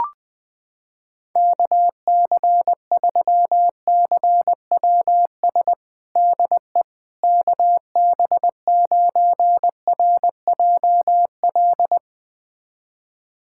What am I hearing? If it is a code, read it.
Morse code, heard as a keyed tone, about 21 wpm, 700 Hz. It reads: KC3CWS DE KB9RJL
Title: USA Callsign DE Callsign - 6 Characters - 20wpm